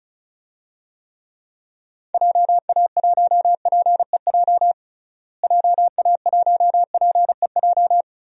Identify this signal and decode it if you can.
Morse code, heard as a keyed tone, about 36 wpm, 700 Hz. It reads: JA1PEJ JA1PEJ
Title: DXCC-100 Call Signs - 6 Characters - Repeated 3x - 35wpm